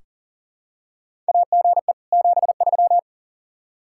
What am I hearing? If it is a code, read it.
Morse code, heard as a keyed tone, about 37 wpm, 700 Hz. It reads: AGE 73
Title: K1USN SST QSO Practice 40wpm